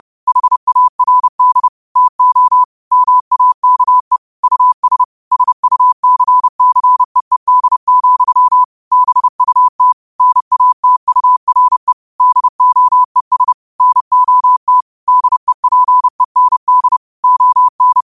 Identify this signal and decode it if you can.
Morse code, heard as a keyed tone, about 30 wpm, 1000 Hz. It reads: HARD TO MAKE US SUCCEED, BUT NATURE DOES NOT DEPEND ON
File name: qotdfiles / 30wpm